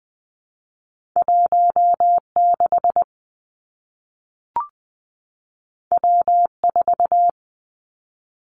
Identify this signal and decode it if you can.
Morse code, heard as a keyed tone, about 20 wpm, 700 Hz. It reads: 1 6 W 4
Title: Two Challenging Letter-Number Combinations 20wpm